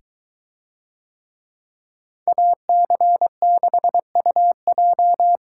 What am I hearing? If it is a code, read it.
Morse code, heard as a keyed tone, about 23 wpm, 700 Hz. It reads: AC6UJ